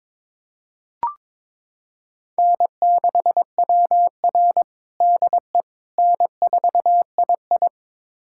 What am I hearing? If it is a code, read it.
Morse code, heard as a keyed tone, about 23 wpm, 700 Hz. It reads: N6WR DE N4II
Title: USA Callsign DE Callsign - 4 Characters - 22wpm